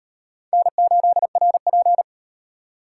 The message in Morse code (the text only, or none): N8RP